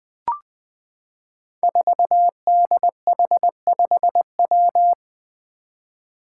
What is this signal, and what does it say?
Morse code, heard as a keyed tone, about 20 wpm, 700 Hz. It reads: 4DH5W
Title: Five Challenging Letter-Number Combinations 20wpm